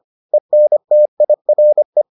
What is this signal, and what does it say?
Morse code, heard as a keyed tone, about 25 wpm, 600 Hz. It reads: ENTIRE